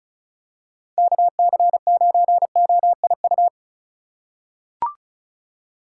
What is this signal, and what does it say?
Morse code, heard as a keyed tone, about 36 wpm, 700 Hz. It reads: KC9OIU
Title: USA Call Signs - 6 Characters - 4x Character Spacing - 35wpm